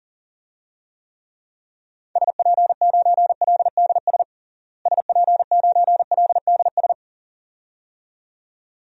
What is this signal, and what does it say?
Morse code, heard as a keyed tone, about 37 wpm, 700 Hz. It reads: SP9LDS SP9LDS
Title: DXCC-100 Call Signs - 6 Characters - Repeated 2x - 40wpm